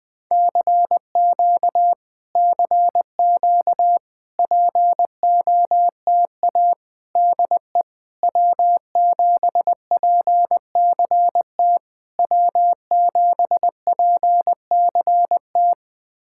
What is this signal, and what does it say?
Morse code, heard as a keyed tone, about 20 wpm, 700 Hz. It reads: CQ CQ POTA DE W7PCT W7PCT
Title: POTA QSO Practice - 20wpm